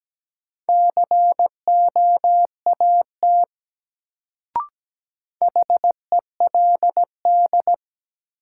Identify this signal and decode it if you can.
Morse code, heard as a keyed tone, about 17 wpm, 700 Hz. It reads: COAT HELD
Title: Sentences from Top 1600 Words - New Word Drill Alt - 17wpm